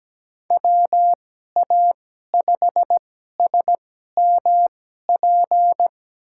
Morse code, heard as a keyed tone, about 17 wpm, spaced as for 12 wpm, 700 Hz. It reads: WA5SMP
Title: USA Call Signs - 2x Character Spacing - 17wpm